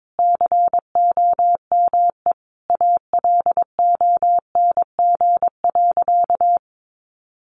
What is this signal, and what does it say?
Morse code, heard as a keyed tone, about 22 wpm, 700 Hz. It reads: COME ALONG.
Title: Sentences from Top 700 Words 22wpm